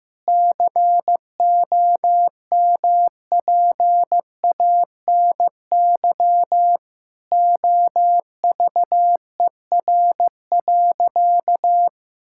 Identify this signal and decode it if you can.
Morse code, heard as a keyed tone, about 15 wpm, 700 Hz. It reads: COMPANY OVER.
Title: Sentences from Top 1000 Words - Review Part 6 - 15wpm